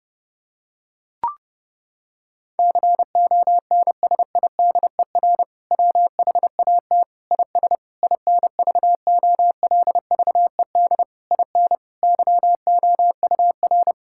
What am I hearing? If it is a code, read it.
Morse code, heard as a keyed tone, about 30 wpm, 700 Hz. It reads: CONSIDER WHAT IS INVOLVED IN YOUR
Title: Sentences from Top 1000 Words - Review Part 5 - 30wpm